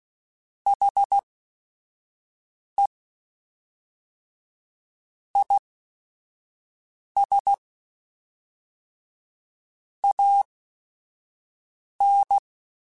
Morse code, heard as a keyed tone, about 16 wpm, spaced as for 5 wpm, 800 Hz. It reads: HE IS AN
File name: Quotes / Quotes-mp3s / Optimist @5